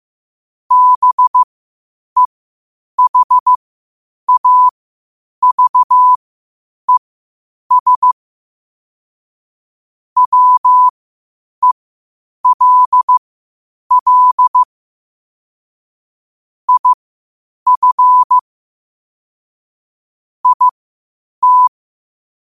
Morse code, heard as a keyed tone, about 15 wpm, spaced as for 8 wpm, 1000 Hz. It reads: BEHAVES WELL IF IT
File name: qotdfiles / 8wpm